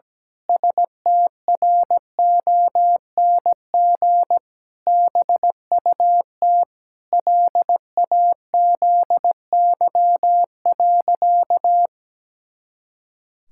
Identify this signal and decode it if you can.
Morse code, heard as a keyed tone, about 17 wpm, 700 Hz. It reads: STRONG BUT LAZY.
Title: Pangram Challenge v2 - 17wpm